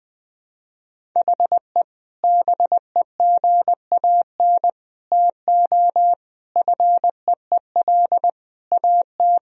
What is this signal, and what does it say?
Morse code, heard as a keyed tone, about 21 wpm, 700 Hz. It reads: HE BEGAN TO FEEL AT
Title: Sentences from Top 1000 Words - Review Part 1 - 20wpm